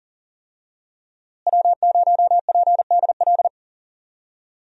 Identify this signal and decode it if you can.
Morse code, heard as a keyed tone, about 40 wpm, 700 Hz. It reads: W0PDL
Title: USA Call Signs - 5 Characters - Repeated 3x - 40wpm